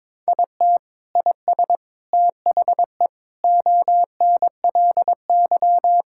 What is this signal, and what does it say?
Morse code, heard as a keyed tone, about 22 wpm, 700 Hz. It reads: IT IS THE ONLY
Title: Sentences from Top 1600 Words 22wpm